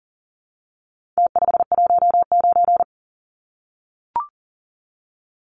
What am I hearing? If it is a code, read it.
Morse code, heard as a keyed tone, about 40 wpm, 700 Hz. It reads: T519
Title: Numbers - Contest Style - 4-Digit 40wpm